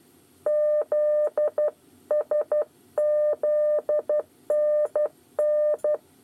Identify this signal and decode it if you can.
Morse code, heard as a keyed tone, about 12 wpm, 600 Hz. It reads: ZSZNN
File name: 8 bis lezione cw 2